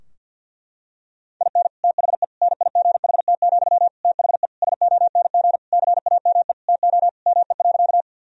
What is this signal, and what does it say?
Morse code, heard as a keyed tone, about 49 wpm, 700 Hz. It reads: IN THE NIGHT, THE SONG CAME TO ME.
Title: Sentences from Top 1000 Words - Review Part 2 - 50wpm